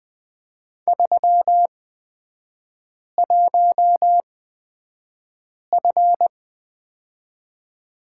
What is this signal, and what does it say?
Morse code, heard as a keyed tone, about 20 wpm, spaced as for 5 wpm, 700 Hz. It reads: 31F
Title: 224 Lesson 36e - BT - Three Characters 20wpm